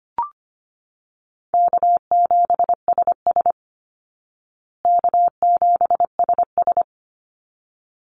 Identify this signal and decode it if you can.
Morse code, heard as a keyed tone, about 25 wpm, 700 Hz. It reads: K7SS K7SS
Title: Easy Call Signs Repeated 2x - 25wpm